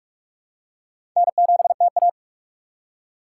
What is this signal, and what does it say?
Morse code, heard as a keyed tone, about 47 wpm, 700 Hz. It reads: N7TA
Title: USA Call Signs - 4x Character Spacing - 45wpm